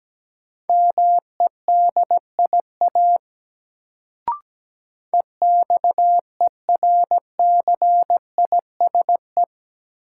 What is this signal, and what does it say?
Morse code, heard as a keyed tone, about 17 wpm, 700 Hz. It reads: MEDIA EXERCISE
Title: Sentences from Top 1000 Words - Review Part 4 - New Word Drill - 17wpm